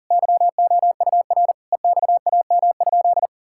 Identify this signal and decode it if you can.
Morse code, heard as a keyed tone, about 40 wpm, 700 Hz. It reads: YOUR EXAM?
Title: Sentences from Top 1000 Words - Review Part 5 - 40wpm